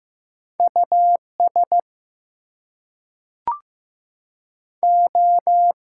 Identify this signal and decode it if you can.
Morse code, heard as a keyed tone, about 15 wpm, 700 Hz. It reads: US O
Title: Two Letter Words - 15wpm